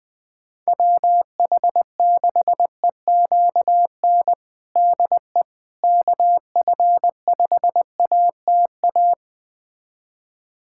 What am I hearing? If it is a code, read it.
Morse code, heard as a keyed tone, about 20 wpm, 700 Hz. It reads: WH6EQN DE KF5ATA
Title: USA Callsign DE Callsign - 6 Characters - 20wpm